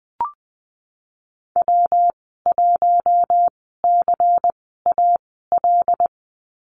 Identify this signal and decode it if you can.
Morse code, heard as a keyed tone, about 20 wpm, spaced as for 15 wpm, 700 Hz. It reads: W1CAL
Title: 154 Lesson 25g - Letter p - Call Signs 20wpm